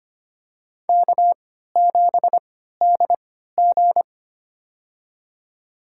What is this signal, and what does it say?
Morse code, heard as a keyed tone, about 25 wpm, spaced as for 14 wpm, 700 Hz. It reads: K7DG
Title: USA Call Signs - 3x Character Spacing - 25wpm